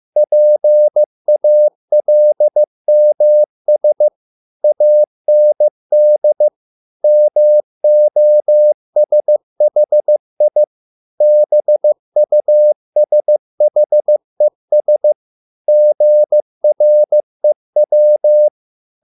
Morse code, heard as a keyed tone, about 15 wpm, 600 Hz. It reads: PALMS AND MOSHI BUSHES GREW